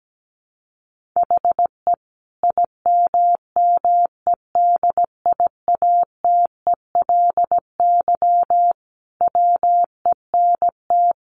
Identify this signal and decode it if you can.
Morse code, heard as a keyed tone, about 17 wpm, 700 Hz. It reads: HE IMMEDIATELY WENT
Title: Sentences from Top 1100 Words 17wpm